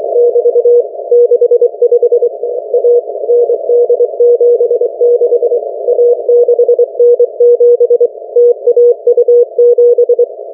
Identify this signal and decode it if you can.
Morse code, heard as a keyed tone, about 24 wpm, 480 Hz. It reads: <BT> 65 A ND76 A6N7 TAU7